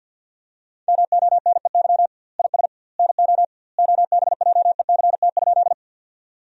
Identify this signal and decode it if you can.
Morse code, heard as a keyed tone, about 53 wpm, 700 Hz. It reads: MONEY IS NO OBJECT?
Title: 2x Word-Spacing - Sentences from Top 500 Words - 50wpm